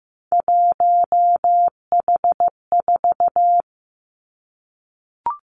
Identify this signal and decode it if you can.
Morse code, heard as a keyed tone, about 15 wpm, 700 Hz. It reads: 1H4
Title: Three Challenging Letter-Number Combinations 15wpm